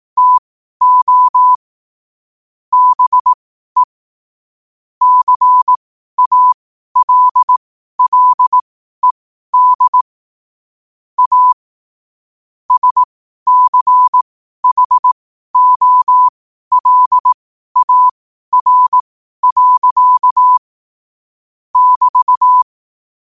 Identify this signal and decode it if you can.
Morse code, heard as a keyed tone, about 18 wpm, spaced as for 12 wpm, 1000 Hz. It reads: TO BE CALLED A SCHOLAR. <BT>
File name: qotdfiles / 12wpm